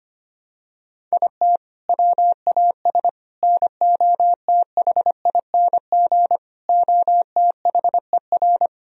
Morse code, heard as a keyed tone, about 25 wpm, 700 Hz. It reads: IT WAS NOTHING OTHER
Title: Sentences from Top 1000 Words - Review Part 2 - 25wpm